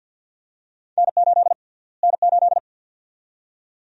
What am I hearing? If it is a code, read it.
Morse code, heard as a keyed tone, about 49 wpm, 700 Hz. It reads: N8 N8
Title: USA Call Sign Prefixes - Repeated 2x - 50wpm